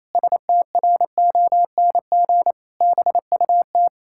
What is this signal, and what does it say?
Morse code, heard as a keyed tone, about 28 wpm, 700 Hz. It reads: STRONG BUT
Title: Pangram Challenge v2 - 28wpm